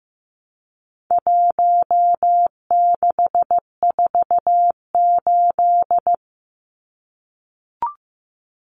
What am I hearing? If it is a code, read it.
Morse code, heard as a keyed tone, about 15 wpm, 700 Hz. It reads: 1648
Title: Numbers - 4-Digit 15wpm